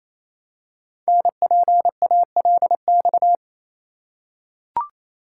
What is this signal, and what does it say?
Morse code, heard as a keyed tone, about 27 wpm, 700 Hz. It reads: NPALX
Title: Code Groups - 5 Letter:Numbers - 28wpm